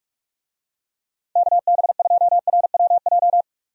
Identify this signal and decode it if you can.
Morse code, heard as a keyed tone, about 44 wpm, 700 Hz. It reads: KB2RWJ
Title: USA Call Signs - 6 Characters - 45wpm